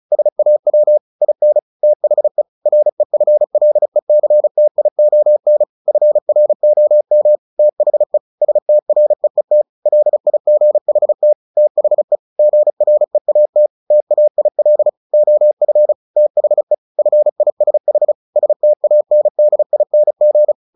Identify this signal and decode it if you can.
Morse code, heard as a keyed tone, about 34 wpm, 600 Hz. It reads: SAW IN THE REFLECTION FROM THE STREET LIGHT THE GREAT TAIL OF THE FISH STANDING
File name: the old man and the sea 2 35wpm